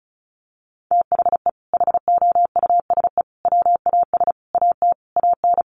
Morse code, heard as a keyed tone, about 34 wpm, 700 Hz. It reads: THE HOUSE WAS AT AN
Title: Sentences from Top 1500 Words 35wpm